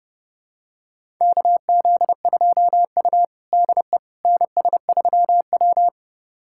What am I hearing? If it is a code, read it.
Morse code, heard as a keyed tone, about 30 wpm, 700 Hz. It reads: KZ2U DE NS3W
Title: USA Callsign DE Callsign - 4 Characters - 30wpm